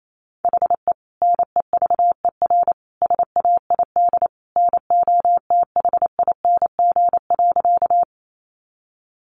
Morse code, heard as a keyed tone, about 28 wpm, 700 Hz. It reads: HE NEVER SAID NOTHING.
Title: Sentences from Top 500 Words 28wpm